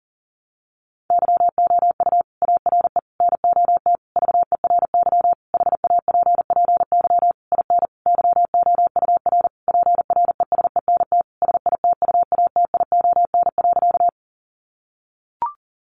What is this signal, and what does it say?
Morse code, heard as a keyed tone, about 40 wpm, 700 Hz. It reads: YOU ARE NOT VERY HAPPY IN YOUR PRESENT SITUATION.
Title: Sentences from Top 1200 Words 40wpm